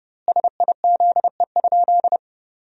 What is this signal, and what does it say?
Morse code, heard as a keyed tone, about 28 wpm, 700 Hz. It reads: SIZE?
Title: 2x Word-Spacing - Sentences from Top 500 Words - 30wpm